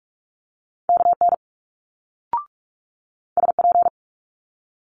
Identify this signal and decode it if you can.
Morse code, heard as a keyed tone, about 46 wpm, 700 Hz. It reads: KN SP
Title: Multi-letter Phonemes Quick Fire 45wpm